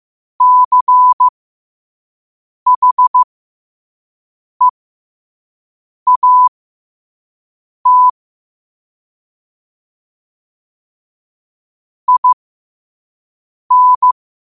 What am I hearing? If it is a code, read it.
Morse code, heard as a keyed tone, about 15 wpm, spaced as for 5 wpm, 1000 Hz. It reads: CHEAT IN
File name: qotdfiles / 5wpm